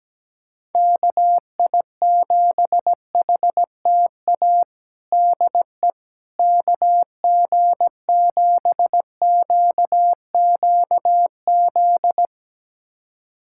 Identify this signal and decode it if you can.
Morse code, heard as a keyed tone, about 17 wpm, 700 Hz. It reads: KI7HTA DE KG7QQZ
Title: USA Callsign DE Callsign - 6 Characters - 17wpm